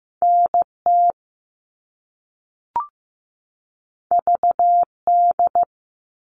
Morse code, heard as a keyed tone, about 15 wpm, 700 Hz. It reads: NT VD
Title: All Cut Numbers - 2-Digit 15wpm